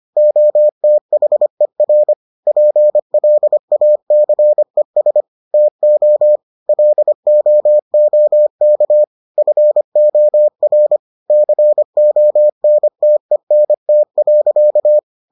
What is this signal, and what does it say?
Morse code, heard as a keyed tone, about 25 wpm, 600 Hz. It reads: OTHER PLACES TO LOOK FOR CONTENT.